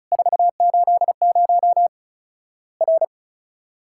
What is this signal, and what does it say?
Morse code, heard as a keyed tone, about 36 wpm, 700 Hz. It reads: 480 R
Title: Worked All Europe DX Contest - QTC Exchange - 35wpm